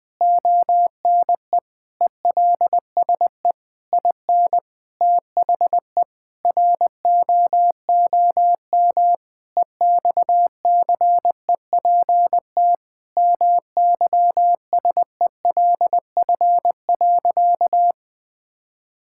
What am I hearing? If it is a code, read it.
Morse code, heard as a keyed tone, about 20 wpm, 700 Hz. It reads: ONE ELSE IN THE ROOM EXCEPT MYSELF.
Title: Sentences from Top 1600 Words 20wpm